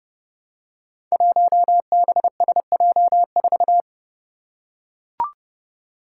Morse code, heard as a keyed tone, about 30 wpm, 700 Hz. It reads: 1BSJ4
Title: Five Challenging Letter-Number Combinations 30wpm